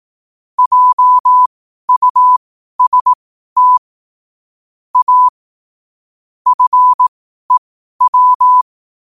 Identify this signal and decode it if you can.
Morse code, heard as a keyed tone, about 18 wpm, spaced as for 12 wpm, 1000 Hz. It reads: JUST A FEW
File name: qotdfiles / 12wpm